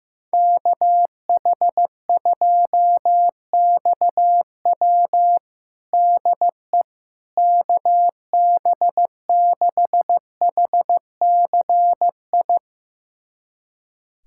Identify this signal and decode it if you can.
Morse code, heard as a keyed tone, about 15 wpm, 700 Hz. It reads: KH2XW DE KB6HCI
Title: USA Callsign DE Callsign 15wpm